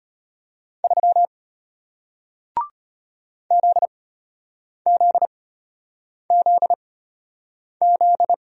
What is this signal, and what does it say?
Morse code, heard as a keyed tone, about 33 wpm, spaced as for 8 wpm, 700 Hz. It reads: 3 ZZZZ